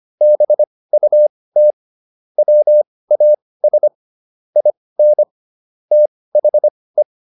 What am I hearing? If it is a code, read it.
Morse code, heard as a keyed tone, about 25 wpm, spaced as for 18 wpm, 600 Hz. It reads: BUT WAS IN THE